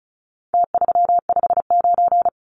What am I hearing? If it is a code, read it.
Morse code, heard as a keyed tone, about 34 wpm, 700 Hz. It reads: T359
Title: Numbers - Contest Style - 4-Digit 35wpm